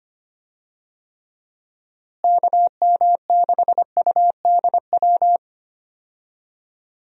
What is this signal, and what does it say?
Morse code, heard as a keyed tone, about 25 wpm, 700 Hz. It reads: KM6UDW
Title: USA Call Signs - 6 Characters - Repeated 3x - 25wpm